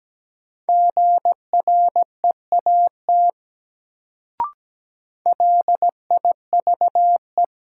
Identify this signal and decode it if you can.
Morse code, heard as a keyed tone, about 17 wpm, 700 Hz. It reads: GREAT LIVE
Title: Sentences from Top 200 Words - New Word Drill - 17wpm